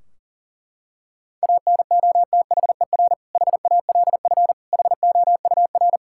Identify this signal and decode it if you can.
Morse code, heard as a keyed tone, about 40 wpm, 700 Hz. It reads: ANOTHER HALF HOUR
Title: Sentences from Top 1600 Words 40wpm